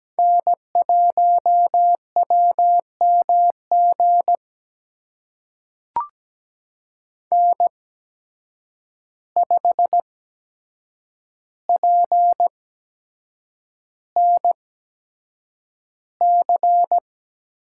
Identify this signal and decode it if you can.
Morse code, heard as a keyed tone, about 17 wpm, spaced as for 7 wpm, 700 Hz. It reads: N1WMG N5PNC
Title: USA Call Signs - 5 Characters - 8x Character Spacing - 17wpm